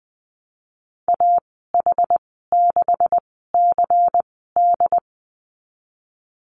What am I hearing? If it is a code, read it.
Morse code, heard as a keyed tone, about 20 wpm, 700 Hz. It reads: A H 6 C D
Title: 154 Lesson 25g - Letter p - Call Signs 20wpm